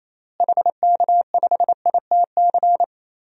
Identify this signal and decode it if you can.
Morse code, heard as a keyed tone, about 28 wpm, 700 Hz. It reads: HK5ITC
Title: DXCC-100 Call Signs - 6 Characters - Repeated 2x - 28wpm